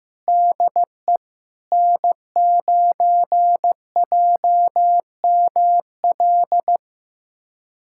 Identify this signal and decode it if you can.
Morse code, heard as a keyed tone, about 16 wpm, 700 Hz. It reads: DE N9JML
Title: USA Callsign DE Callsign - 5 Characters - 15wpm